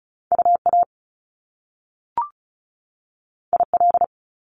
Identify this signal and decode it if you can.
Morse code, heard as a keyed tone, about 36 wpm, 700 Hz. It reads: UA IL
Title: Top Two Letter Combinations in Words 35wpm